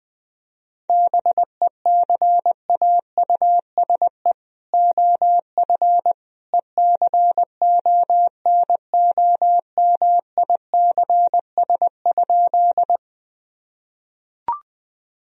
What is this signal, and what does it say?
Morse code, heard as a keyed tone, about 20 wpm, 700 Hz. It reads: BECAUSE OF ECONOMICS?
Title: Sentences from Top 1000 Words - Review Part 3 - 20wpm